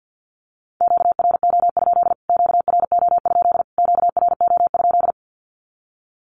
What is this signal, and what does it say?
Morse code, heard as a keyed tone, about 49 wpm, 700 Hz. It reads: QRO? QRO? QRO?
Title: Q-Codes Repeated 3x 50wpm